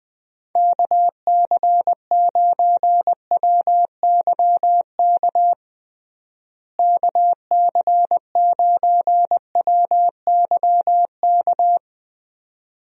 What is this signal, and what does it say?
Morse code, heard as a keyed tone, about 21 wpm, 700 Hz. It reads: KC9WYK KC9WYK
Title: USA Call Signs - 6 Characters - Repeated 2x - 20wpm